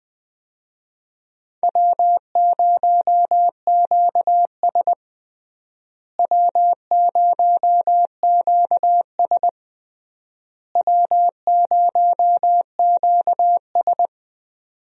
Morse code, heard as a keyed tone, about 19 wpm, 700 Hz. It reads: W0QS W0QS W0QS
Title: USA Call Signs - 4 Characters - Repeated 3x - 20wpm